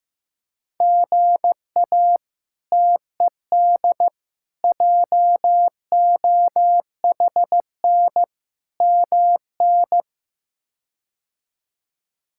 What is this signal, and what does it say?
Morse code, heard as a keyed tone, about 16 wpm, 700 Hz. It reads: GA TED JOHN MN
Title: K1USN SST QSO Practice - 15wpm